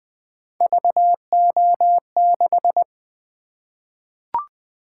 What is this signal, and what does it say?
Morse code, heard as a keyed tone, about 20 wpm, 700 Hz. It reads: VO6
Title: Three Letter-Number Combinations - 20wpm